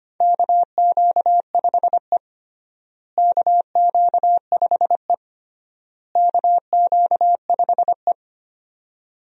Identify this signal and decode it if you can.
Morse code, heard as a keyed tone, about 25 wpm, 700 Hz. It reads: KQ5E KQ5E KQ5E
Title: USA Call Signs - 4 Characters - Repeated 3x - 25wpm